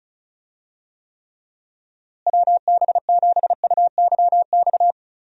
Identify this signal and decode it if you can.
Morse code, heard as a keyed tone, about 36 wpm, 700 Hz. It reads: WB7UYX
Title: USA Call Signs - 6 Characters - Repeated 3x - 35wpm